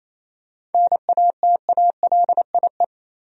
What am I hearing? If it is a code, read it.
Morse code, heard as a keyed tone, about 28 wpm, 700 Hz. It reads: NATALIE
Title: USA Common Names - 28wpm